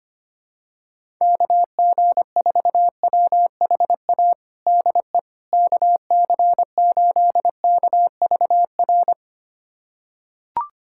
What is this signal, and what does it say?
Morse code, heard as a keyed tone, about 25 wpm, 700 Hz. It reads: KG4WHA DE KC8KVR
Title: USA Callsign DE Callsign - 6 Characters - 25wpm